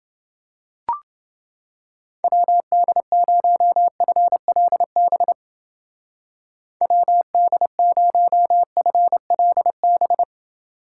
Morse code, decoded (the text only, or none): WD0FLB WD0FLB